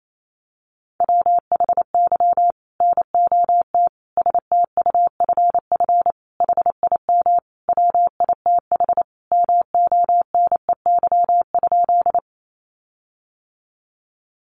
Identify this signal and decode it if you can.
Morse code, heard as a keyed tone, about 28 wpm, 700 Hz. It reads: WHY NOT STUFF HIM WITH MONEY?
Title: Sentences from Top 1400 Words 28wpm